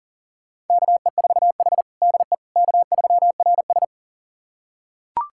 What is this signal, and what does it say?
Morse code, heard as a keyed tone, about 40 wpm, 690 Hz. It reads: KE4H DE K3RS
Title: USA Callsign DE Callsign - 4 Characters - 40wpm